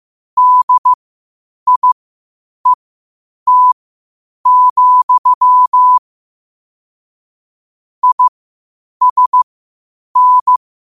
Morse code, heard as a keyed tone, about 15 wpm, spaced as for 8 wpm, 1000 Hz. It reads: DIET, ISN
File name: qotdfiles / 8wpm